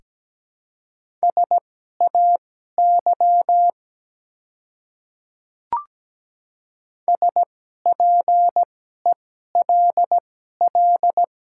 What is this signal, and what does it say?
Morse code, heard as a keyed tone, about 17 wpm, spaced as for 12 wpm, 690 Hz. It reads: SAY SPELL
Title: 2x Character Spacing - Top 300 Words - 17wpm